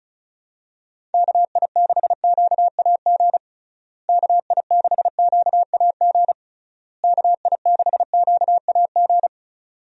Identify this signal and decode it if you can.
Morse code, heard as a keyed tone, about 36 wpm, 700 Hz. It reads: KI6QAG KI6QAG KI6QAG
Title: USA Call Signs - 6 Characters - Repeated 3x - 35wpm